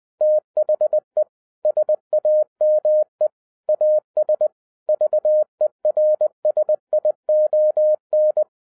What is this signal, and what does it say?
Morse code, heard as a keyed tone, about 20 wpm, 610 Hz. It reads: THE SAME AS VERSION